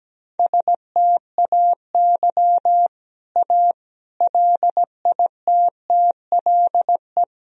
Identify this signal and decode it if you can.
Morse code, heard as a keyed tone, about 17 wpm, 700 Hz. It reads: STAY A LITTLE
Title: Sentences from Top 1000 Words - Review Part 1 - 17wpm